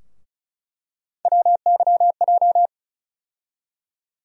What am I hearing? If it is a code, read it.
Morse code, heard as a keyed tone, about 36 wpm, 700 Hz. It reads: WYJ
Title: USA Call Sign Postfixes - 3 Letters 35wpm